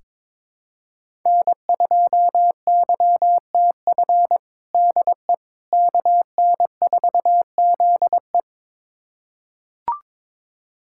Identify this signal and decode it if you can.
Morse code, heard as a keyed tone, about 22 wpm, 700 Hz. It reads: N2YTF DE KN4ZE
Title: USA Callsign DE Callsign - 5 Characters - 22wpm